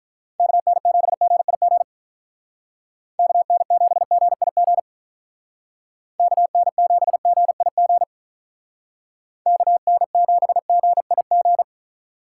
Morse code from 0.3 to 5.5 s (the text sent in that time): KN7GIG KN7GIG